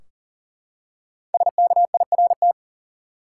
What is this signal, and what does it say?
Morse code, heard as a keyed tone, about 38 wpm, 700 Hz. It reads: SKIRT
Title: Five Letter Words - 40wpm